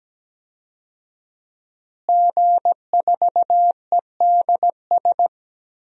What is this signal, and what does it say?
Morse code, heard as a keyed tone, about 17 wpm, 700 Hz. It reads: G4EDS
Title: DXCC-100 Call Signs - 5 Characters - Repeated 2x - 17wpm